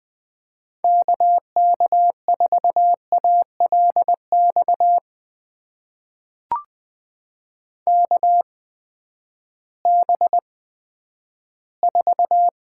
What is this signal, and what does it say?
Morse code, heard as a keyed tone, about 20 wpm, spaced as for 11 wpm, 700 Hz. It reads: KK4ALX KB4
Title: USA Call Signs - 6 Characters - 8x Character Spacing - 20wpm